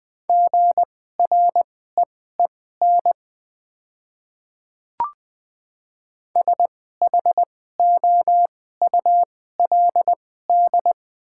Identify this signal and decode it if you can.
Morse code, heard as a keyed tone, about 20 wpm, spaced as for 14 wpm, 700 Hz. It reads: GREEN SHOULD
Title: 2x Character Spacing - Top 500 Words - 20wpm